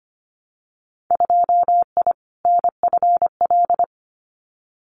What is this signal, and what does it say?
Morse code, heard as a keyed tone, about 25 wpm, 700 Hz. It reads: 2I NFL
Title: Winter Field Day Exchanges 25wpm